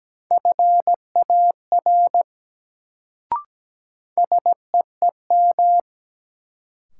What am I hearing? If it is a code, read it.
Morse code, heard as a keyed tone, about 17 wpm, 700 Hz. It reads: FAR SEEM
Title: Sets of 3 Words - New Word Drill - 17wpm